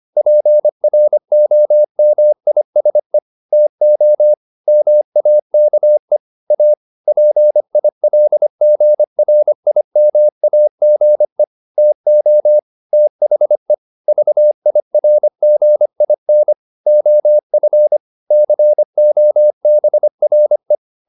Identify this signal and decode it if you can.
Morse code, heard as a keyed tone, about 25 wpm, 600 Hz. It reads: PROMISE TO MAKE A PILGRIMAGE TO THE VIRGIN OF COBRE
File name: the old man and the sea part 1 25wpm